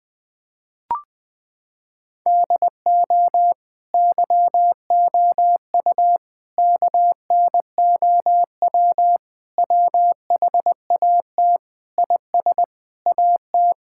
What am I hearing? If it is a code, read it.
Morse code, heard as a keyed tone, about 20 wpm, 700 Hz. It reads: DO YOU KNOW WHAT IS AT
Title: Sentences from Top 1000 Words - Review Part 6 - 20wpm